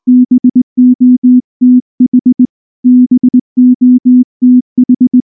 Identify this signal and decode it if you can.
Morse code, heard as a keyed tone, about 19 wpm, 250 Hz. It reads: BOTH BOTH
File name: morse / morse-training-20WPM-7